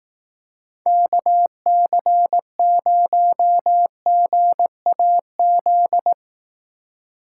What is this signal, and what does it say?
Morse code, heard as a keyed tone, about 18 wpm, 700 Hz. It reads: KC0GAZ
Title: Speed Racing - USA 2x3 Call Signs 15wpm